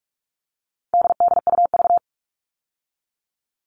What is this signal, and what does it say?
Morse code, heard as a keyed tone, about 44 wpm, 700 Hz. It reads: DDUV
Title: All Cut Numbers - 4-Digit 45wpm